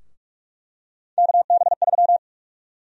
Morse code, heard as a keyed tone, about 46 wpm, 700 Hz. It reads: KB3
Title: USA Call Sign Prefixes - Repeated 2x - 45wpm